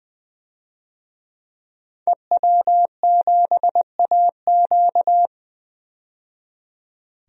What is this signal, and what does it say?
Morse code, heard as a keyed tone, about 20 wpm, 700 Hz. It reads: EW7AQ
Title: DXCC-100 Call Signs - 5 Characters - 20wpm